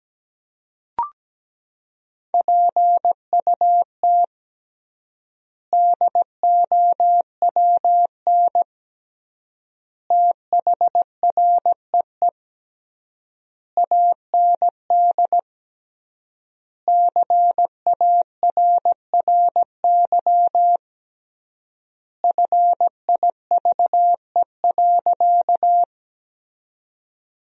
Morse code, heard as a keyed tone, about 17 wpm, 700 Hz. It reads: PUT DOWN THREE AND CARRY FIVE.
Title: 3x Word-Spacing - Sentences from Top 400 Words - 17wpm